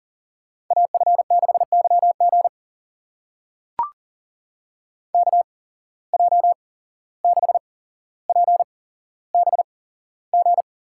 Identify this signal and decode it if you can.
Morse code, heard as a keyed tone, about 40 wpm, spaced as for 16 wpm, 700 Hz. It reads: AF6YG KJ6PBG
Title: USA Call Signs - 8x Character Spacing - 40wpm